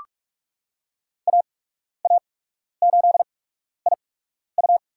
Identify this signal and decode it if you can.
Morse code, heard as a keyed tone, about 46 wpm, spaced as for 12 wpm, 700 Hz. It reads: AA8IU
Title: USA Call Signs - 5 Characters - 8x Character Spacing - 45wpm